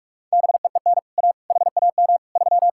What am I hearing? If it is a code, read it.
Morse code, heard as a keyed tone, about 59 wpm, spaced as for 52 wpm, 700 Hz. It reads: BEEN A HAM 3
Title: Speed-Racing Common QSO Phrases - 45wpm